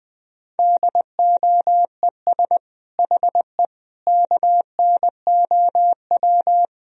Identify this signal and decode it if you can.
Morse code, heard as a keyed tone, about 20 wpm, 700 Hz. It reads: DOES HE KNOW
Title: Sentences from Top 1000 Words - Review Part 2 - 20wpm